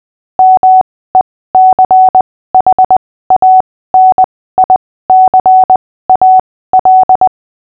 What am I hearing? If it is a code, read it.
Morse code, heard as a keyed tone, about 20 wpm, spaced as for 15 wpm, 750 Hz. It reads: MECHANICAL
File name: cw / fists / Short-Story-6-15wpm